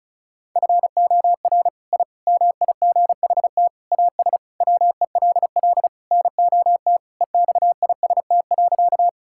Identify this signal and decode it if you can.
Morse code, heard as a keyed tone, about 34 wpm, 700 Hz. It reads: FOR I MIGHT AS WELL NOT EXIST.
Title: Sentences from Top 1000 Words - Review Part 5 - 35wpm